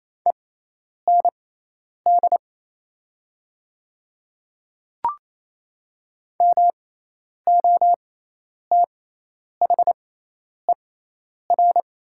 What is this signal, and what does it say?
Morse code, heard as a keyed tone, about 27 wpm, spaced as for 10 wpm, 700 Hz. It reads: END MOTHER
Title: 6x Character Spacing - Top 200 Words - 28wpm